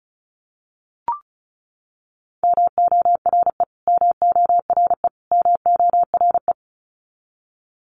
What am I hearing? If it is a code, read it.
Morse code, heard as a keyed tone, about 34 wpm, 700 Hz. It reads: MORE MORE MORE
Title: Top 300 Words Repeated 35wpm